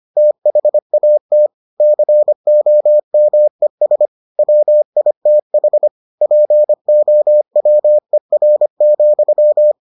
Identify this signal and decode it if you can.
Morse code, heard as a keyed tone, about 25 wpm, 600 Hz. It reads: THAT COMES WITH POWER,